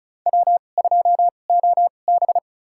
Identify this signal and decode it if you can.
Morse code, heard as a keyed tone, about 34 wpm, spaced as for 25 wpm, 700 Hz. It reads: W2OB
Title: USA Call Signs - 4 Characters - 2x Character Spacing - 35wpm